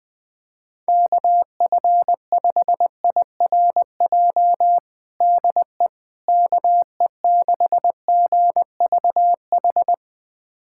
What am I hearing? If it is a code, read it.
Morse code, heard as a keyed tone, about 21 wpm, 700 Hz. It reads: KF5IRJ DE KE6GVH
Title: USA Callsign DE Callsign - 6 Characters - 20wpm